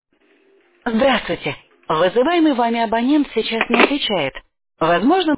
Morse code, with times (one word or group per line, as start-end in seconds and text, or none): none